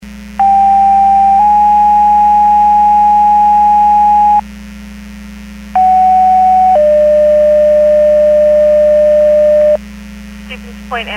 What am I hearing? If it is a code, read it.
Two-tone page, heard as a one-second tone, then a three-second tone, about 790 then 830 Hz, about 750 then 600 Hz, with speech. It is heard from 0.4 to 9.8 s.